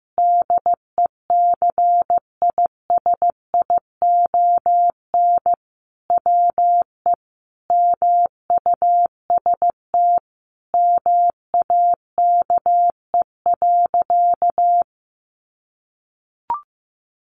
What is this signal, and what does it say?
Morse code, heard as a keyed tone, about 15 wpm, 700 Hz. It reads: DECISION WE MUST MAKE.
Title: Sentences from Top 1000 Words 15wpm